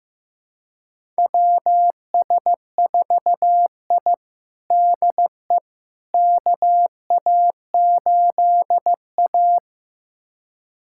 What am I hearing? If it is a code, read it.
Morse code, heard as a keyed tone, about 16 wpm, 700 Hz. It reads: WS4I DE KA8A
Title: USA Callsign DE Callsign - 4 Characters - 15wpm